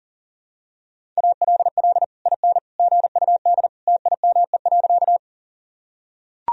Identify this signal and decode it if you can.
Morse code, heard as a keyed tone, about 40 wpm, 700 Hz. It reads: ALL IN GUD TIME.
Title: Sentences from Top 200 Words and Abbreviations 40wpm